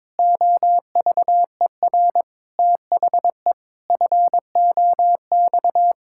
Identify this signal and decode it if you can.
Morse code, heard as a keyed tone, about 22 wpm, 700 Hz. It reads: OVER THE FOX
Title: Pangram Challenge v2 - 22wpm